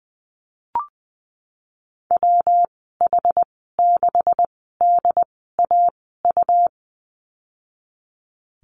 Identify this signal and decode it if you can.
Morse code, heard as a keyed tone, about 20 wpm, spaced as for 14 wpm, 700 Hz. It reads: WH6DAU
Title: 128 Lesson 21g - Number 6 - Call Signs 20wpm